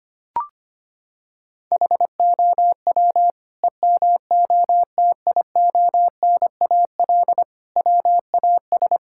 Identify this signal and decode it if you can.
Morse code, heard as a keyed tone, about 25 wpm, 700 Hz. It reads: HOW EMOTIONAL WAS
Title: Sentences from Top 1000 Words - Review Part 5 - 25wpm